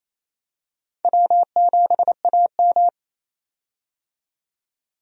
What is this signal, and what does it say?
Morse code, heard as a keyed tone, about 28 wpm, 700 Hz. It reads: W7AM
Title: USA Call Signs - 4 Characters - 28wpm